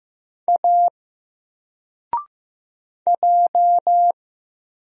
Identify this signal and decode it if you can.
Morse code, heard as a keyed tone, about 15 wpm, 700 Hz. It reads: A J
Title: Single Letter-Number 15wpm